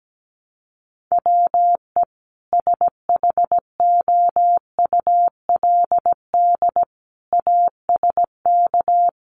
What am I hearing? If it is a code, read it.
Morse code, heard as a keyed tone, about 17 wpm, 700 Hz. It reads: WE SHOULD ASK
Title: Sentences from Top 300 Words 17wpm